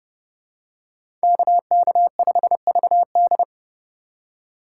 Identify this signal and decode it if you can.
Morse code, heard as a keyed tone, about 28 wpm, 700 Hz. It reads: KK5VD